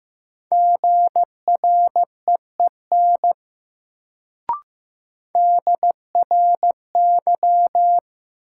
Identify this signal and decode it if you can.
Morse code, heard as a keyed tone, about 15 wpm, 700 Hz. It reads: GREEN DRY
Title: Sentences from Top 500 Words - New Word Drill - 15wpm